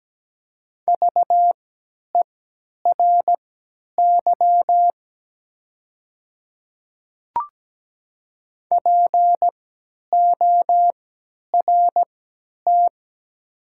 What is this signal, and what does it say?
Morse code, heard as a keyed tone, about 17 wpm, spaced as for 10 wpm, 690 Hz. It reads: VERY PORT
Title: 3x Character Spacing - Top 300 Words - 17wpm